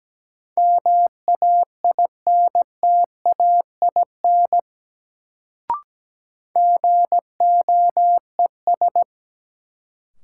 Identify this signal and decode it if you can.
Morse code, heard as a keyed tone, about 17 wpm, 700 Hz. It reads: MAINTAIN GOES
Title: Sentences from Top 1000 Words - Review Part 5 - New Word Drill - 17wpm